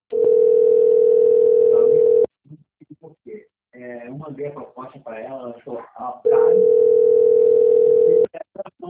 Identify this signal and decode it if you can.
Ringback tone (North America), 440 + 480 Hz, with speech.
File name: rg-1600-unknown- - - .32674